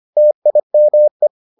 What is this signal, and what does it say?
Morse code, heard as a keyed tone, about 25 wpm, 600 Hz. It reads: TIME